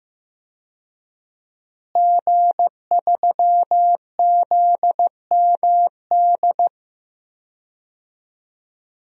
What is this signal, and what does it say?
Morse code, heard as a keyed tone, about 15 wpm, 700 Hz. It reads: G3ZMD
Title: DXCC-100 Call Signs - 5 Characters - Repeated 2x - 15wpm